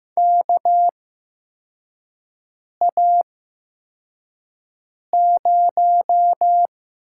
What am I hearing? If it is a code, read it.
Morse code, heard as a keyed tone, about 15 wpm, spaced as for 4 wpm, 700 Hz. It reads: KA0
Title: USA Call Signs - 6 Characters - 8x Character Spacing - 15wpm